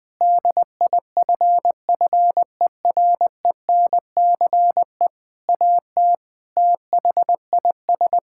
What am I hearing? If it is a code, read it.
Morse code, heard as a keyed tone, about 20 wpm, 700 Hz. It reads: DIFFERENCE AT THIS
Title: Sentences with 8-Word Phrases - 20wpm